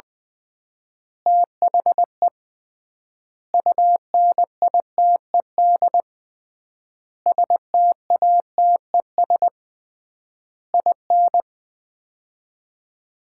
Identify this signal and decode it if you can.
Morse code, heard as a keyed tone, about 20 wpm, 700 Hz. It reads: THE UNITED STATES IN
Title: 3x Word-Spacing - Sets of 5 Words - 20wpm